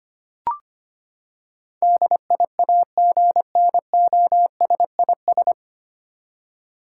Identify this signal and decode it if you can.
Morse code, heard as a keyed tone, about 25 wpm, 700 Hz. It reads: DIAGNOSIS
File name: Nine Letter Words - 25wpm